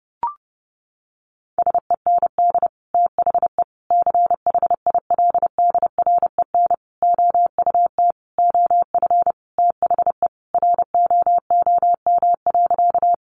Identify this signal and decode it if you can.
Morse code, heard as a keyed tone, about 30 wpm, 700 Hz. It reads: SEND THE CHILDREN OUT OF THE ROOM.
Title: Sentences from Top 1500 Words 30wpm